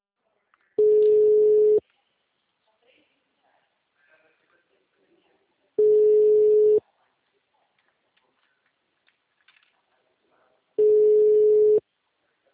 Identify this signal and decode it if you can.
Ringback tone (Europe, 425 Hz), with speech.
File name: out- -2015- - - .90872